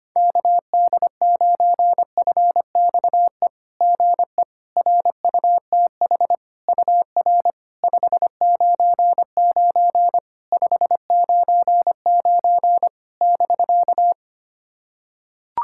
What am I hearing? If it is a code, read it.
Morse code, heard as a keyed tone, about 25 wpm, 690 Hz. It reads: KD9FXE GE RUTH UR 599 599 <BK>
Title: SOTA QSO Practice 25wpm